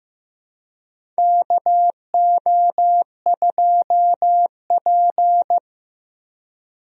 Morse code, heard as a keyed tone, about 15 wpm, 700 Hz. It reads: KO2P
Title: USA Call Signs - 4 Characters - Repeated 2x - 15wpm